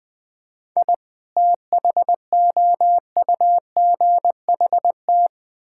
Morse code, heard as a keyed tone, about 20 wpm, 700 Hz. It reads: I THOUGHT
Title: Build A Phrase - 5-Word Phrases - 20wpm